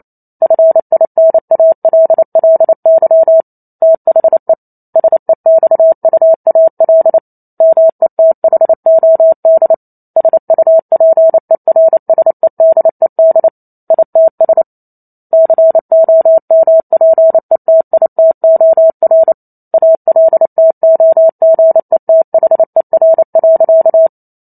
Morse code, heard as a keyed tone, about 28 wpm, 650 Hz. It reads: FINALLY THE SEXUAL METHOD SUPERSEDED ITS COMPETITOR ALTOGETHER.